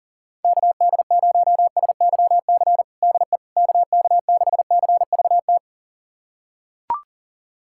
Morse code, heard as a keyed tone, about 40 wpm, 700 Hz. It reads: KD0SYC DE KK6CVT
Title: USA Callsign DE Callsign 40wpm